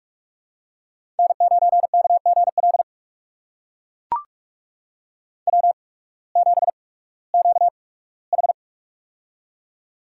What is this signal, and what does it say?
Morse code, heard as a keyed tone, about 43 wpm, spaced as for 21 wpm, 700 Hz. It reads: N9KGL W7QH
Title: USA Call Signs - 8x Character Spacing - 45wpm